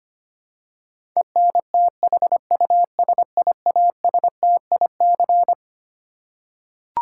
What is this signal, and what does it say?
Morse code, heard as a keyed tone, about 25 wpm, 700 Hz. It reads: ENTHUSIASTIC
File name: Twelve Letter Words - 25wpm